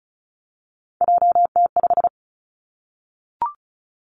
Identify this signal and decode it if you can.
Morse code, heard as a keyed tone, about 36 wpm, 700 Hz. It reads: JT5
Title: Three Challenging Letter-Number Combinations 35wpm